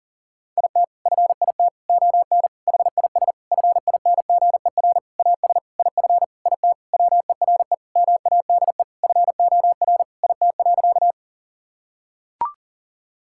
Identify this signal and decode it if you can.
Morse code, heard as a keyed tone, about 40 wpm, 700 Hz. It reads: IT FIT ON HIS FINGER AS IF IT WERE MADE FOR IT.
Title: Sentences from Top 1600 Words 40wpm